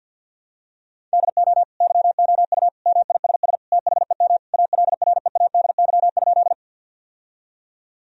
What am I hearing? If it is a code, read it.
Morse code, heard as a keyed tone, about 50 wpm, 700 Hz. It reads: DO YOU MISS THEM ALREADY?
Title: Sentences from Top 1000 Words - Review Part 3 - 50wpm